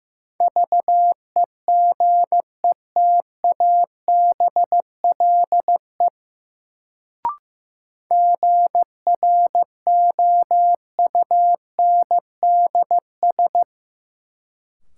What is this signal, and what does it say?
Morse code, heard as a keyed tone, about 15 wpm, 700 Hz. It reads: VEGETABLE GROUNDS
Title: Sentences from Top 1300 Words - New Word Drill Alt - 15wpm